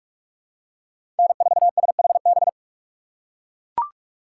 Morse code, heard as a keyed tone, about 44 wpm, 700 Hz. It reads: N4SHB
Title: USA Call Signs - 5 Characters - 8x Character Spacing - 45wpm